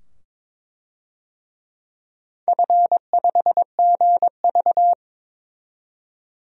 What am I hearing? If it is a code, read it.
Morse code, heard as a keyed tone, about 22 wpm, 700 Hz. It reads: F5GV
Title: DXCC-100 Call Signs - 4 Characters - 22wpm